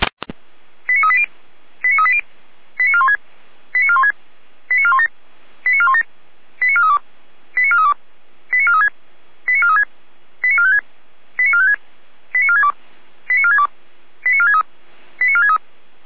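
Five-tone selective call (ZVEI), heard as quick runs of five tones, tones of about 70 ms. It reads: 89380 89380 89416 89416 89417 89417 89432 89432 89436 89436 89456 89456 89462 89462 89463 89463
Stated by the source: ZVEI five-tone selective call received via SDR.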